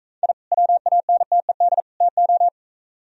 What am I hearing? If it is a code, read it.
Morse code, heard as a keyed tone, about 41 wpm, 700 Hz. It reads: I WANTED TO